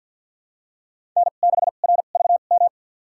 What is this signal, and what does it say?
Morse code, heard as a keyed tone, about 51 wpm, spaced as for 36 wpm, 700 Hz. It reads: N6RVM
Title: USA Call Signs - 5 Characters - 2x Character Spacing - 50wpm